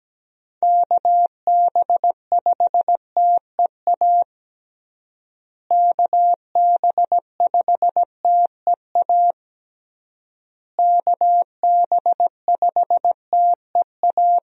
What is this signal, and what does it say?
Morse code, heard as a keyed tone, about 17 wpm, 700 Hz. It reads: KB5TEA KB5TEA KB5TEA
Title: USA Call Signs - 6 Characters - Repeated 3x - 17wpm